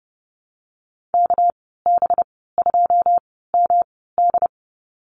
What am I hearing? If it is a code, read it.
Morse code, heard as a keyed tone, about 30 wpm, spaced as for 17 wpm, 700 Hz. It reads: KB2MD
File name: Lesson 37g - Letter x - Call Signs 30wpm